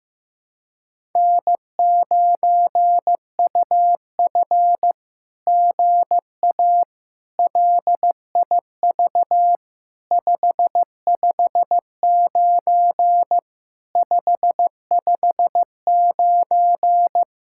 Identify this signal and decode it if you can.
Morse code, heard as a keyed tone, about 15 wpm, 700 Hz. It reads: N9UF GA LIV 559 559
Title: SOTA QSO Practice 15wpm